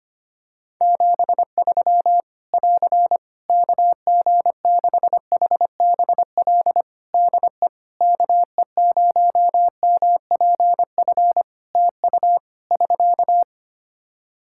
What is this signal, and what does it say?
Morse code, heard as a keyed tone, about 25 wpm, 700 Hz. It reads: 73 <AR> KG6HBL DE KE0MPF TU <SK>
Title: Practice QSO - Exchange 4 - Ending - 25wpm